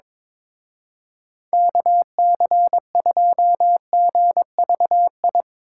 Morse code, heard as a keyed tone, about 22 wpm, 700 Hz. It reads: KC2GVI
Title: Speed Racing - USA 2x3 Call Signs 22wpm